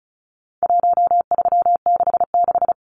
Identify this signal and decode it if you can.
Morse code, heard as a keyed tone, about 34 wpm, 700 Hz. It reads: 1366
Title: Numbers - Contest Style - 4-Digit 35wpm